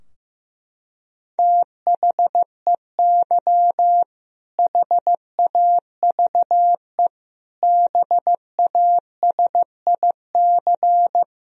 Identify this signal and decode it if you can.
Morse code, heard as a keyed tone, about 15 wpm, 700 Hz. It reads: THEY HAVE BASIC
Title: Sentences from Top 1000 Words - Review Part 3 - 15wpm